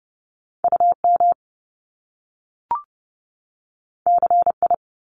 Morse code, heard as a keyed tone, about 30 wpm, 700 Hz. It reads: UM CI
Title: Top Two Letter Combinations in Words 30wpm